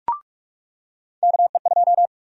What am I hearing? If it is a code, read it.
Morse code, heard as a keyed tone, about 44 wpm, 700 Hz. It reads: KE2
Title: USA Call Sign Prefixes - 45wpm